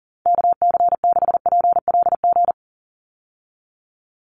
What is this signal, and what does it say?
Morse code, heard as a keyed tone, about 40 wpm, 700 Hz. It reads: KC6PLG
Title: USA Call Signs 40wpm